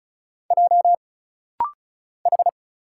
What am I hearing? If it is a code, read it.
Morse code, heard as a keyed tone, about 36 wpm, 700 Hz. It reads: J H
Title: Single Letter-Number - Rapid-Fire 35wpm